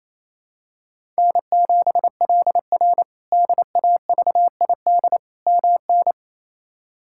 Morse code, heard as a keyed tone, about 28 wpm, 700 Hz. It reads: N7LR DAVID MN
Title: K1USN SST QSO Practice - 28wpm